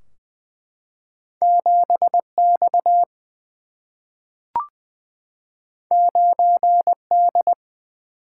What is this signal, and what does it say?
Morse code, heard as a keyed tone, about 20 wpm, 700 Hz. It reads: 7X 9D
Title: Single-Letter Single-Number 20wpm